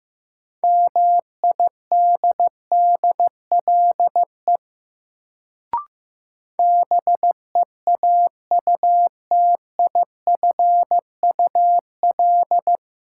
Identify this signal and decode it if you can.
Morse code, heard as a keyed tone, about 15 wpm, 700 Hz. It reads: MIDDLE BEAUTIFUL
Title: Sentences from Top 900 Words - New Word Drill - 15wpm